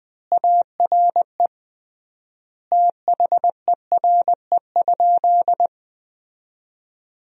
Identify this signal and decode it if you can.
Morse code, heard as a keyed tone, about 20 wpm, 700 Hz. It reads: ARE THERE?
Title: 3x Word-Spacing - Sentences from Top 100 Words - 20wpm